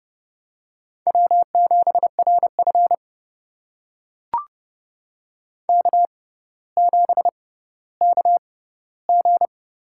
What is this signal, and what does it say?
Morse code, heard as a keyed tone, about 30 wpm, spaced as for 15 wpm, 700 Hz. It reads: W7RF K7KG
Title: USA Call Signs - 6x Character Spacing - 30wpm